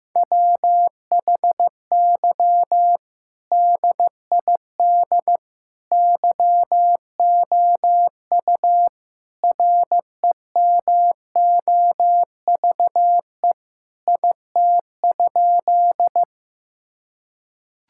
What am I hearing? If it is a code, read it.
Morse code, heard as a keyed tone, about 15 wpm, 700 Hz. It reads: WHY DID YOU REMOVE IT?
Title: Sentences from Top 1000 Words - Review Part 5 - 15wpm